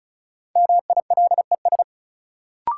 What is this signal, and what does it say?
Morse code, heard as a keyed tone, about 36 wpm, 700 Hz. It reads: MILES
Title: Sentences from Top 1000 Words - Review Part 3 - New Word Drill - 35wpm